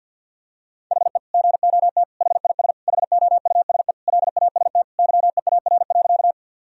Morse code, heard as a keyed tone, about 51 wpm, 700 Hz. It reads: HE GOT HIS HOUSE LAST YEAR.
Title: Sentences from Top 1000 Words - Review Part 1 - 50wpm